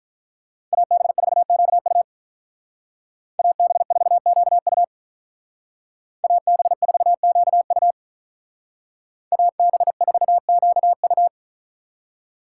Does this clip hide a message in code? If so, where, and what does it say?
Yes: Morse code from 0.7 to 11.3 s, heard as a keyed tone, about 46 wpm, 700 Hz. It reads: AB4QU AB4QU AB4QU AB4QU